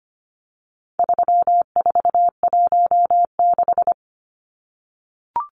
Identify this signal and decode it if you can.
Morse code, heard as a keyed tone, about 25 wpm, 700 Hz. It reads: 3416
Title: Numbers - 4-Digit 25wpm